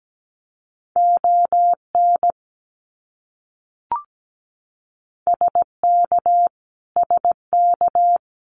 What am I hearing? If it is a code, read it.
Morse code, heard as a keyed tone, about 17 wpm, 700 Hz. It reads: ON SK SK
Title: Canadian Provinces and Territories - Repeated 3x - 17wpm